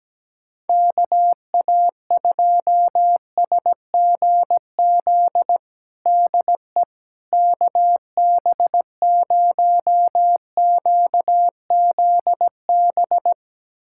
Morse code, heard as a keyed tone, about 17 wpm, 700 Hz. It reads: KA2SGZ DE KB0QZB
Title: USA Callsign DE Callsign 17wpm